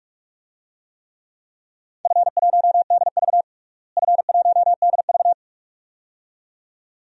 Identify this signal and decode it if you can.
Morse code, heard as a keyed tone, about 46 wpm, 700 Hz. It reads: F1DV F1DV
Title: DXCC-100 Call Signs - 4 Characters - Repeated 2x - 45wpm